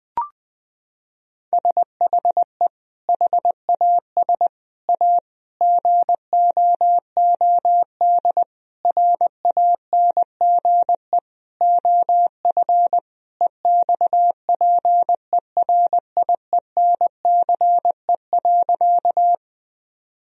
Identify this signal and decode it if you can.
Morse code, heard as a keyed tone, about 20 wpm, 700 Hz. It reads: SHE HAS A GOOD RANGE OF EXPERIENCE.
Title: Sentences from Top 1000 Words - Review Part 4 - 20wpm